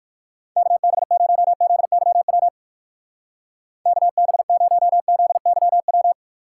Morse code, heard as a keyed tone, about 58 wpm, 700 Hz. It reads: KB0ZYW KB0ZYW